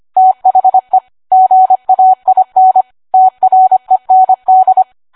Morse code, heard as a keyed tone, about 25 wpm, 750 Hz. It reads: THE GAIN TREND